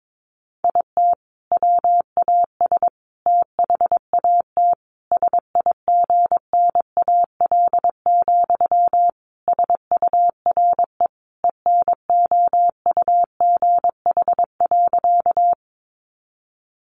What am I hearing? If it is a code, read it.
Morse code, heard as a keyed tone, about 22 wpm, 700 Hz. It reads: IT WAS THAT SIGNAL, SURE ENOUGH.
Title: Sentences from Top 1400 Words 22wpm